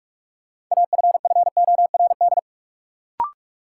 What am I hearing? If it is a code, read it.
Morse code, heard as a keyed tone, about 41 wpm, 700 Hz. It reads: AFFORD
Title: Sentences from Top 1100 Words - New Word Drill Alt - 45wpm